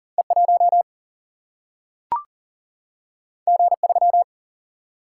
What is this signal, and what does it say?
Morse code, heard as a keyed tone, about 40 wpm, 700 Hz. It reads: E 1 G 3
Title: Single-Letter Single-Number 40wpm